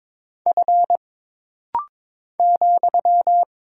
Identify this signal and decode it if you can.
Morse code, heard as a keyed tone, about 22 wpm, 700 Hz. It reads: F ,
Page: This rapid-fire timing is geared to push your ICR (Instant Character Recognition) skills.